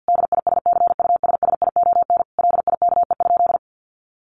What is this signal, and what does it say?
Morse code, heard as a keyed tone, about 50 wpm, 700 Hz. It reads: DISCUSSION LIKE?
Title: Sentences from Top 1500 Words 50wpm